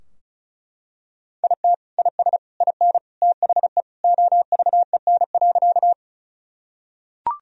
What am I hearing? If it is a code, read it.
Morse code, heard as a keyed tone, about 34 wpm, 700 Hz. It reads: IT IS IN THE OVEN.
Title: Sentences from Top 1000 Words - Review Part 4 - 35wpm